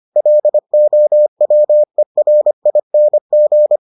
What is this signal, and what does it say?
Morse code, heard as a keyed tone, about 25 wpm, 600 Hz. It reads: LOWERING